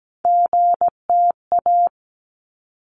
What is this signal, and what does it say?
Morse code, heard as a keyed tone, about 17 wpm, 700 Hz. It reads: GTA
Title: ARRL Sections 17wpm